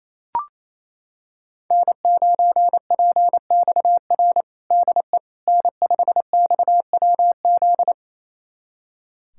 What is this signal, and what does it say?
Morse code, heard as a keyed tone, about 28 wpm, 700 Hz. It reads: N9PXR DE N5XWZ
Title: USA Callsign DE Callsign - 5 Characters - 28wpm